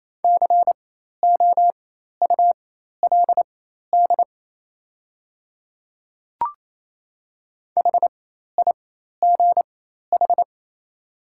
Morse code, heard as a keyed tone, about 28 wpm, spaced as for 13 wpm, 690 Hz. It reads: COULD HIGH
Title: 4x Character Spacing - Top 300 Words - 28wpm